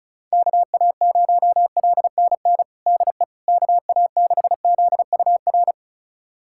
Morse code, heard as a keyed tone, about 36 wpm, 700 Hz. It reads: KA0LNN DE KA6ZUR
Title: USA Callsign DE Callsign - 6 Characters - 35wpm